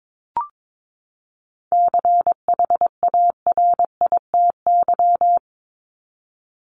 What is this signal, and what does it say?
Morse code, heard as a keyed tone, about 22 wpm, 700 Hz. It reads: CHARITY
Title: Top 1501-1750 Words 22wpm